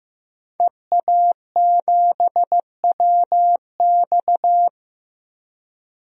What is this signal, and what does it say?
Morse code, heard as a keyed tone, about 15 wpm, 700 Hz. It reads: EA7WX
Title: DXCC-100 Call Signs - 5 Characters - 15wpm